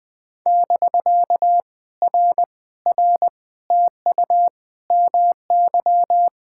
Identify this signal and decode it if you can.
Morse code, heard as a keyed tone, about 20 wpm, 700 Hz. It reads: <BK> R R TU MY
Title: POTA QSO Practice - 20wpm